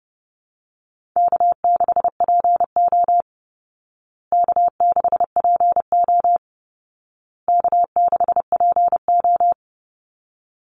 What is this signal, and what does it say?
Morse code, heard as a keyed tone, about 28 wpm, 700 Hz. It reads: K6PO K6PO K6PO
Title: Call Signs Repeated 2x - 30wpm